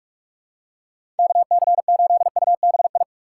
Morse code, heard as a keyed tone, about 44 wpm, 700 Hz. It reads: KC8UBI
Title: USA Call Signs - 6 Characters - 6x Character Spacing - 45wpm